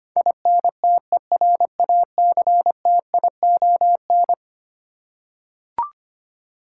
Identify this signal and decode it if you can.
Morse code, heard as a keyed tone, about 25 wpm, 700 Hz. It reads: INTERACTION
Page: morse-code-ninja-practice-25wpm